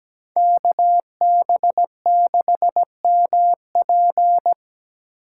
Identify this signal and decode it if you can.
Morse code, heard as a keyed tone, about 20 wpm, 700 Hz. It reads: KB6MP
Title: Speed Racing - USA 2x2 Call Signs 17wpm